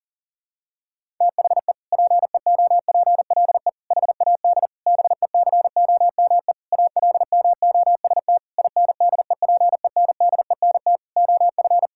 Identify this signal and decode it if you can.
Morse code, heard as a keyed tone, about 40 wpm, 700 Hz. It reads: THE PEOPLE HAD BECOME ALMOST INDEPENDENT OF
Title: Sentences from Top 1000 Words - Review Part 6 - 40wpm